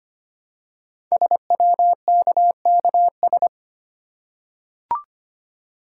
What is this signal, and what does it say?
Morse code, heard as a keyed tone, about 25 wpm, 700 Hz. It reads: SWKKS
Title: Code Groups - 5 Letters - 25wpm